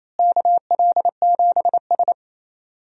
Morse code, heard as a keyed tone, about 28 wpm, 700 Hz. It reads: KL7S
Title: DXCC-100 Call Signs - 4 Characters - Repeated 3x - 28wpm